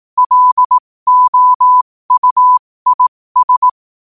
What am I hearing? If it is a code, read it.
Morse code, heard as a keyed tone, about 18 wpm, spaced as for 15 wpm, 1000 Hz. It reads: LOUIS
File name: qotdfiles / 15wpm